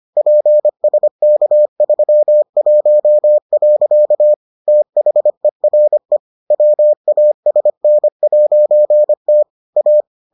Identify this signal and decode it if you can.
Morse code, heard as a keyed tone, about 25 wpm, 600 Hz. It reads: PSK31. THERE WASN'T A